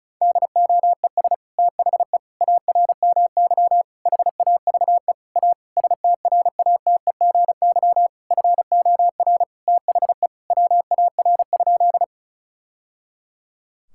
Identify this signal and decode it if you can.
Morse code, heard as a keyed tone, about 36 wpm, 700 Hz. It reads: DOES THE ARMY HAVE A STRATEGY FOR THE WAR?
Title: Sentences from Top 1000 Words - Review Part 5 - 35wpm